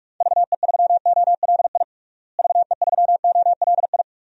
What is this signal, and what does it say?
Morse code, heard as a keyed tone, about 42 wpm, 700 Hz. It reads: VE3OLI VE3OLI
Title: DXCC-100 Call Signs - 6 Characters - Repeated 2x - 45wpm